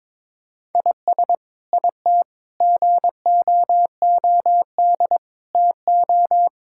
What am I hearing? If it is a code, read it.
Morse code, heard as a keyed tone, about 21 wpm, 700 Hz. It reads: IS IT GOOD TO
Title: Sentences from Top 1000 Words - Review Part 6 - 22wpm